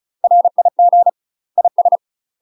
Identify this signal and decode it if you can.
Morse code, heard as a keyed tone, about 34 wpm, 700 Hz. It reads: RIG IS
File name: UK MASTER QSOs TRAINING / 35WPM / UK MASTER QSOs 0040